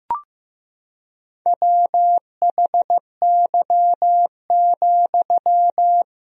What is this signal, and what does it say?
Morse code, heard as a keyed tone, about 15 wpm, 700 Hz. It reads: WHY,
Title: 4x Word-Spacing - Sentences from Top 300 Words - 15wpm